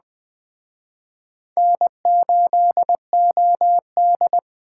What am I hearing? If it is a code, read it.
Morse code, heard as a keyed tone, about 20 wpm, 700 Hz. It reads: N8OD
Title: Speed Racing - USA 1x2 Call Signs 17wpm